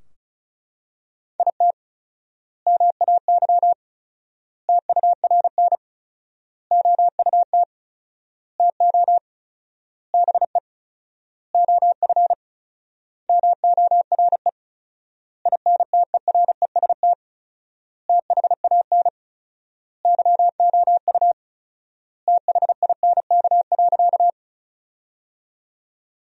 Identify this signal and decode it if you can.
Morse code, heard as a keyed tone, about 36 wpm, 700 Hz. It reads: IT MAY TURN OUT TO BE OF MORE INTEREST THAN YOU THINK.
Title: 4x Word-Spacing - Sentences from Top 400 Words - 35wpm